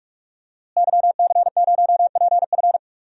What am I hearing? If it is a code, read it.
Morse code, heard as a keyed tone, about 46 wpm, 700 Hz. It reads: YC0PF
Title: DXCC-100 Call Signs - 5 Characters - Repeated 2x - 45wpm